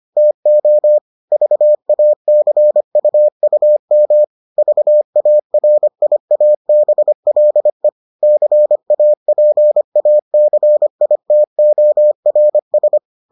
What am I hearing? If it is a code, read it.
Morse code, heard as a keyed tone, about 25 wpm, 600 Hz. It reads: TO VACUUM VARIABLE CAPACITORS